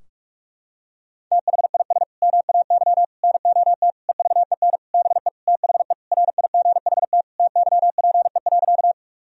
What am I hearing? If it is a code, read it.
Morse code, heard as a keyed tone, about 46 wpm, 700 Hz. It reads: THIS MAY NOT EVEN BE THE RIGHT TYPE.
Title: Sentences from Top 1000 Words - Review Part 3 - 45wpm